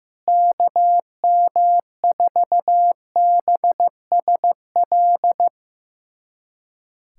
Morse code, heard as a keyed tone, about 15 wpm, 700 Hz. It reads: KM4BSL
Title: USA Callsign DE Callsign - 6 Characters - 15wpm